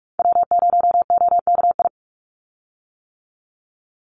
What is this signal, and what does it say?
Morse code, heard as a keyed tone, about 44 wpm, 700 Hz. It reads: W0OKI
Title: USA Call Signs 45wpm